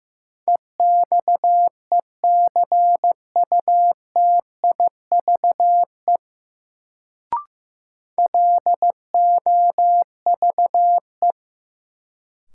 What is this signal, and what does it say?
Morse code, heard as a keyed tone, about 15 wpm, 700 Hz. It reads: EXECUTIVE LOVE
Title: Sets of 5 Words Encore - New Word Drill - 15wpm